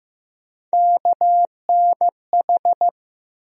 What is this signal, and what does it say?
Morse code, heard as a keyed tone, about 15 wpm, 700 Hz. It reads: KNH
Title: USA Call Sign Postfixes - 3 Letters 15wpm